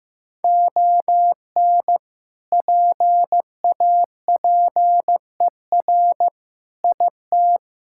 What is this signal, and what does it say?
Morse code, heard as a keyed tone, about 15 wpm, 700 Hz. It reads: ON PAPER IT
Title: Sentences from Top 1600 Words 15wpm